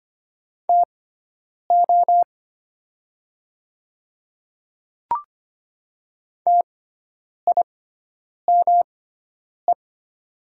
Morse code, heard as a keyed tone, about 25 wpm, spaced as for 9 wpm, 700 Hz. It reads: TO TIME
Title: 6x Character Spacing - Top 100 Words - 25wpm